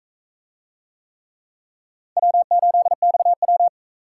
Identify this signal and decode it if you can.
Morse code, heard as a keyed tone, about 38 wpm, 700 Hz. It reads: W8XW